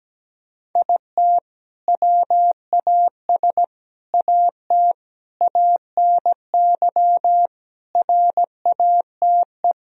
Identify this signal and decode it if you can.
Morse code, heard as a keyed tone, about 17 wpm, 700 Hz. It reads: IT WAS AT ANY RATE
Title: Sentences from Top 1000 Words - Review Part 4 - 17wpm